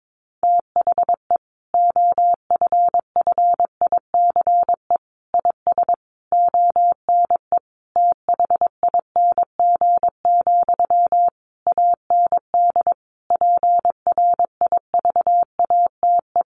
Morse code, heard as a keyed tone, about 22 wpm, 700 Hz. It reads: THE OFFICE IS ONE THING, AND PRIVATE
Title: Sentences from Top 1000 Words 22wpm